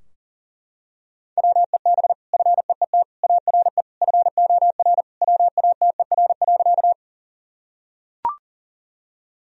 Morse code, heard as a keyed tone, about 40 wpm, 700 Hz. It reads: WEB FEET ARE FOR WATER.
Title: Sentences from Top 1000 Words - Review Part 4 - 40wpm